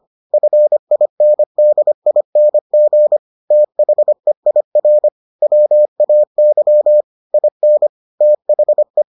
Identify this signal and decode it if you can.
Morse code, heard as a keyed tone, about 25 wpm, 600 Hz. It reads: FINDING THEIR WAY IN THE